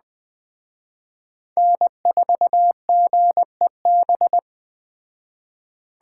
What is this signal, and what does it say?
Morse code, heard as a keyed tone, about 20 wpm, 700 Hz. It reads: N4GEB
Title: Speed Racing - USA Call Signs 17wpm